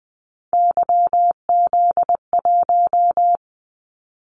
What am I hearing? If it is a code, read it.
Morse code, heard as a keyed tone, about 20 wpm, 700 Hz. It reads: YZ1
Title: Callsign Prefixes for Top 25 DXCC - 20wpm